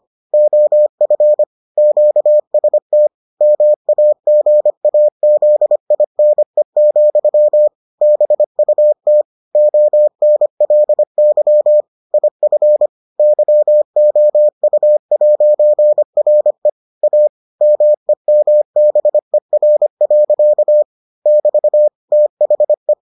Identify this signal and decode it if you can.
Morse code, heard as a keyed tone, about 25 wpm, 600 Hz. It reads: OF QST MAGAZINE, BUT ONLY IF YOU'RE A MEMBER. <BT> THE